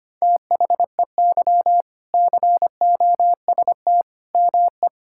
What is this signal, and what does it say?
Morse code, heard as a keyed tone, about 25 wpm, 700 Hz. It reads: THEY COST ME
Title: Sentences from Top 1000 Words - Review Part 5 - 25wpm